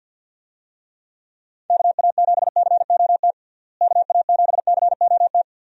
Morse code, heard as a keyed tone, about 49 wpm, 700 Hz. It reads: KA7COT KA7COT
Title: USA Call Signs - 6 Characters - Repeated 2x - 50wpm